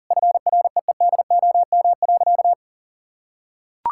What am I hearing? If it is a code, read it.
Morse code, heard as a keyed tone, about 40 wpm, 700 Hz. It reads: FREEDOM.
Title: Sentences from Top 1000 Words - Review Part 5 - 40wpm